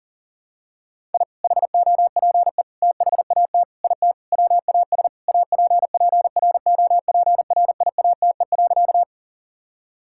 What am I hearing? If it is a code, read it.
Morse code, heard as a keyed tone, about 40 wpm, 700 Hz. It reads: I HOPE THAT IT WAS APPROPRIATE.
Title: Sentences from Top 1000 Words - Review Part 5 - 40wpm